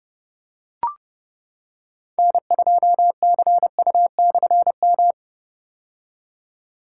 Morse code, heard as a keyed tone, about 31 wpm, 700 Hz. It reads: N2CU/M
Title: K1USN SST QSO Practice - 30wpm